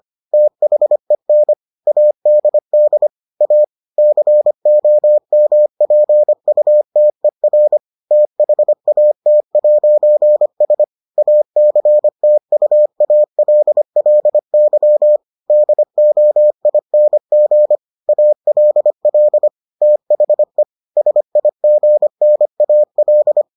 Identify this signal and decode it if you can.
Morse code, heard as a keyed tone, about 25 wpm, 600 Hz. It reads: THEN ADD A COMPUTER THAT'S ACTUALLY DOING ALL THE SIGNAL